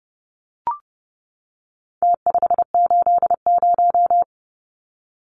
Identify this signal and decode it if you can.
Morse code, heard as a keyed tone, about 30 wpm, 700 Hz. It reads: T580
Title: Numbers - Contest Style - 4-Digit 30wpm